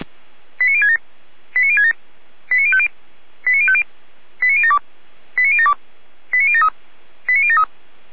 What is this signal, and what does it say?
Five-tone selective call (ZVEI), heard as quick runs of five tones, tones of about 70 ms. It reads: 89067 89067 89050 89050 89072 89072 89073 89073